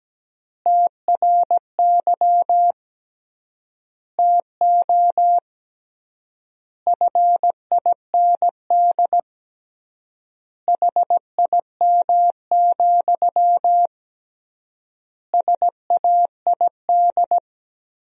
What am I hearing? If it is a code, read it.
Morse code, heard as a keyed tone, about 17 wpm, 700 Hz. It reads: TRY TO FIND HIM, SAID
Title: 3x Word-Spacing - Sentences from Top 400 Words - 17wpm